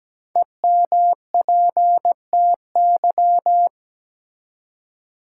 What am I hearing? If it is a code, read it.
Morse code, heard as a keyed tone, about 17 wpm, 700 Hz. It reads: EMPTY
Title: Five Letter Words - 17wpm